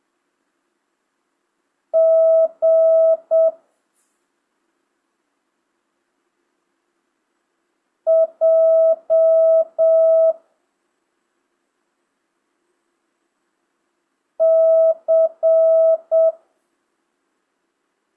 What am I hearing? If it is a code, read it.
Morse code, heard as a keyed tone, about 7 wpm, 650 Hz. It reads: G J C
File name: jh8cbh / 2025cw